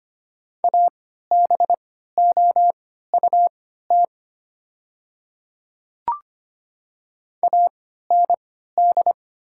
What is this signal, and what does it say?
Morse code, heard as a keyed tone, about 25 wpm, spaced as for 14 wpm, 700 Hz. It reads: ABOUT AND
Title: 3x Character Spacing - Top 200 Words - 25wpm